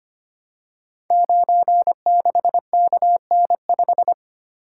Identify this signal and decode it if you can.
Morse code, heard as a keyed tone, about 25 wpm, 700 Hz. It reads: 96KN5
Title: Code Groups - 5 Letter:Numbers - 25wpm